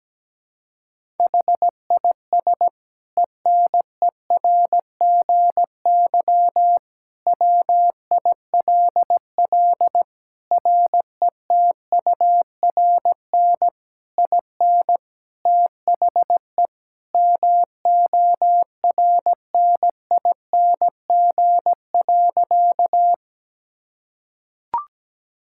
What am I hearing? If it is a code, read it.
Morse code, heard as a keyed tone, about 17 wpm, 700 Hz. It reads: HIS ENERGY WILL RETURN IN THE MORNING.
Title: Sentences from Top 1000 Words - Review Part 6 - 17wpm